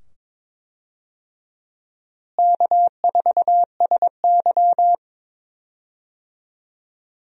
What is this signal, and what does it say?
Morse code, heard as a keyed tone, about 22 wpm, 700 Hz. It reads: K4SY
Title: USA Call Signs - 4 Characters - Repeated 2x - 22wpm